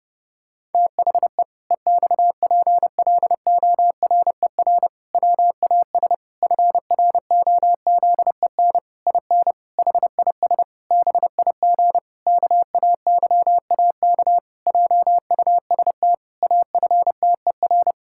Morse code, heard as a keyed tone, about 30 wpm, 700 Hz. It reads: THE EXPLORER WAS FROZEN IN HIS BIG KAYAK JUST AFTER
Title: Pangram Challenge v2 - 30wpm